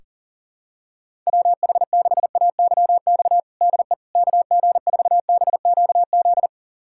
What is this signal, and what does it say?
Morse code, heard as a keyed tone, about 40 wpm, 700 Hz. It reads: WH6AYX DE KG4BQZ
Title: USA Callsign DE Callsign 40wpm